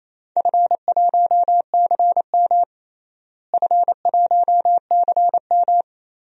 Morse code, heard as a keyed tone, about 28 wpm, 700 Hz. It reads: F1CM F1CM
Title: DXCC-100 Call Signs - 4 Characters - Repeated 2x - 28wpm